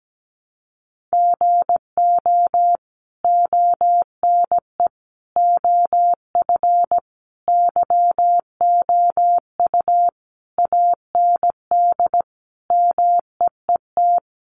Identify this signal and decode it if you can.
Morse code, heard as a keyed tone, about 17 wpm, 700 Hz. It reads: GO ONE OF YOU AND MEET
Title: Sentences from Top 1100 Words 17wpm